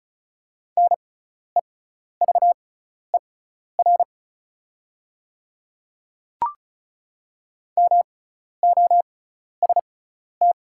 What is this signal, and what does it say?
Morse code, heard as a keyed tone, about 36 wpm, spaced as for 12 wpm, 690 Hz. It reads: NEVER MOST
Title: 6x Character Spacing - Top 300 Words - 35wpm